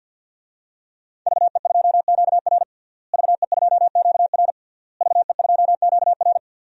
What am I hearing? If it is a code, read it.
Morse code, heard as a keyed tone, about 52 wpm, 700 Hz. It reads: VE2QR VE2QR VE2QR
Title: DXCC-100 Call Signs - 5 Characters - Repeated 3x - 50wpm